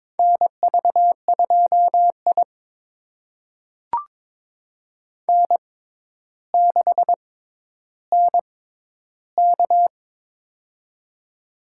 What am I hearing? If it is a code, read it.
Morse code, heard as a keyed tone, about 22 wpm, spaced as for 11 wpm, 700 Hz. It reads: NV2I N6NK
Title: USA Call Signs - 4 Characters - 6x Character Spacing - 22wpm